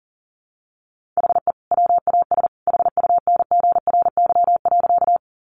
Morse code, heard as a keyed tone, about 40 wpm, 700 Hz. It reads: HE WAS HUNGRY.
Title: Sentences from Top 1400 Words 40wpm